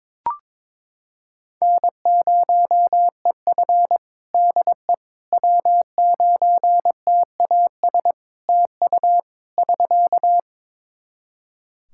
Morse code, heard as a keyed tone, about 22 wpm, 700 Hz. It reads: N0EF DE W9TAS TU <SK>
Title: Practice QSO - Exchange 4 - Ending - 22wpm